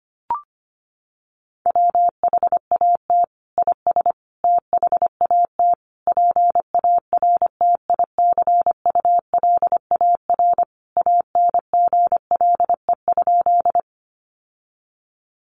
Morse code, heard as a keyed tone, about 25 wpm, 700 Hz. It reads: WHAT IS THAT PARTICULAR ANGLE?
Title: Sentences from Top 1500 Words 25wpm